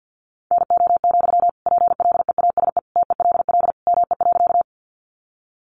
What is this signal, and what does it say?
Morse code, heard as a keyed tone, about 50 wpm, 700 Hz. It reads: NO, PLEASE TELL ME.
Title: Sentences from Top 1500 Words 50wpm